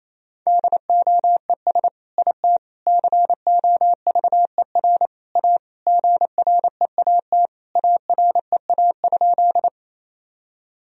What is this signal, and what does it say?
Morse code, heard as a keyed tone, about 28 wpm, 700 Hz. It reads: DOES IT COVER A GREAT AREA?
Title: Sentences from Top 1000 Words - Review Part 2 - 28wpm